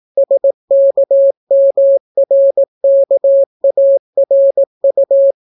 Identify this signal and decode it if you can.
Morse code, heard as a keyed tone, about 18 wpm, 550 Hz. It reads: SKMRKARU